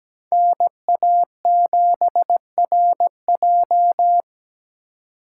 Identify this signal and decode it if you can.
Morse code, heard as a keyed tone, about 20 wpm, 700 Hz. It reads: NA7RJ
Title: Speed Racing - USA 2x2 Call Signs 17wpm